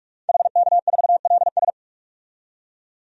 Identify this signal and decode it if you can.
Morse code, heard as a keyed tone, about 46 wpm, 700 Hz. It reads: HK4LS
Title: DXCC-100 Call Signs - 5 Characters - 45wpm